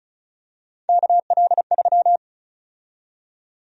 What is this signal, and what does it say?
Morse code, heard as a keyed tone, about 34 wpm, 700 Hz. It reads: KL3
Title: USA Call Sign Prefixes - Repeated 2x - 35wpm